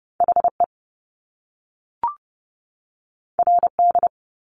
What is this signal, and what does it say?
Morse code, heard as a keyed tone, about 30 wpm, 700 Hz. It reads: HE RD
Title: Top Two Letter Combinations in Words 30wpm